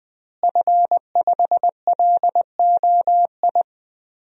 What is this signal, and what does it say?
Morse code, heard as a keyed tone, about 20 wpm, 700 Hz. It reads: F5LOI
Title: DXCC-100 Call Signs - 5 Characters - Repeated 3x - 20wpm